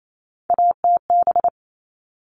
Morse code, heard as a keyed tone, about 28 wpm, 700 Hz. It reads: ATB
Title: All Cut Numbers - 3-Digit 28wpm